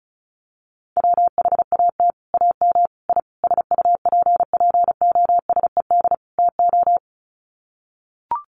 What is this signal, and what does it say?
Morse code, heard as a keyed tone, about 34 wpm, 700 Hz. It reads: WHAT AM I SUPPOSED TO
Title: Sets of 5 Words Encore2 35wpm